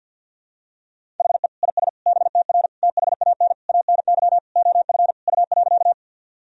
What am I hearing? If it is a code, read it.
Morse code, heard as a keyed tone, about 52 wpm, 700 Hz. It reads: HE IS BTR THAN ANY OF U.
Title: Sentences from Top 400 Words and Abbreviations 50wpm